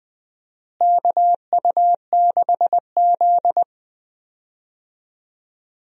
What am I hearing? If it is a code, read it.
Morse code, heard as a keyed tone, about 20 wpm, 700 Hz. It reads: KU6Z
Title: USA Call Signs - 4 Characters - Repeated 3x - 20wpm